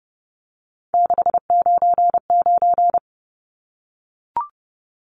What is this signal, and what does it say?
Morse code, heard as a keyed tone, about 30 wpm, 700 Hz. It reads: 699
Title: Numbers - 3-Digit 30wpm